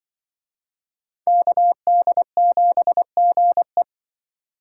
Morse code, heard as a keyed tone, about 24 wpm, 700 Hz. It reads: KD7GE
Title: Speed Racing - USA Call Signs 20wpm